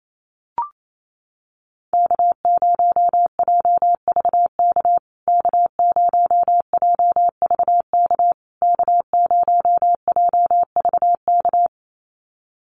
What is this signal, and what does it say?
Morse code, heard as a keyed tone, about 28 wpm, 700 Hz. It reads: K0JVK K0JVK K0JVK
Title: USA Call Signs Repeated 3x 28wpm